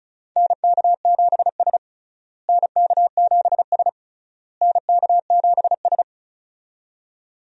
Morse code, heard as a keyed tone, about 36 wpm, 700 Hz. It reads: NK7S NK7S NK7S
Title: USA Call Signs - 4 Characters - Repeated 3x - 35wpm